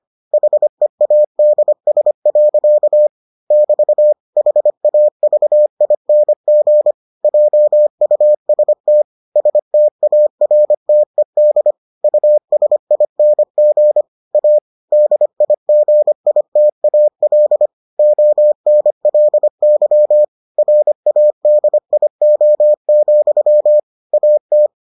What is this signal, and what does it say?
Morse code, heard as a keyed tone, about 25 wpm, 600 Hz. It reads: HEADS. <BT> HAVING JUST STARTED USING A DIGITAL ONLY RADIO, AT